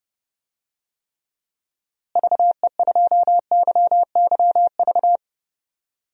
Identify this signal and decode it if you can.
Morse code, heard as a keyed tone, about 32 wpm, 700 Hz. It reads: VE2YYV
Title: DXCC-100 Call Signs - 6 Characters - Repeated 2x - 30wpm